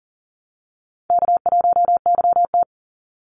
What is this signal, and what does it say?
Morse code, heard as a keyed tone, about 40 wpm, 700 Hz. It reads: K1YT
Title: Easy Call Signs Repeated 2x - 40wpm